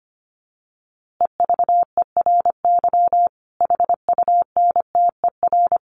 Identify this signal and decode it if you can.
Morse code, heard as a keyed tone, about 25 wpm, 700 Hz. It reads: EVERY HUNTER
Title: A Sound Of Thunder 25wpm